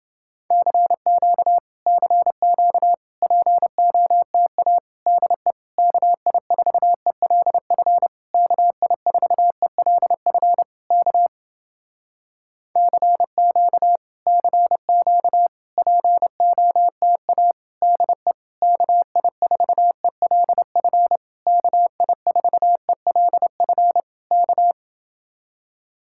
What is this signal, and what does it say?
Morse code, heard as a keyed tone, about 29 wpm, 700 Hz. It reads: CQ CQ POTA DE KI4ELF KI4ELF K CQ CQ POTA DE KI4ELF KI4ELF K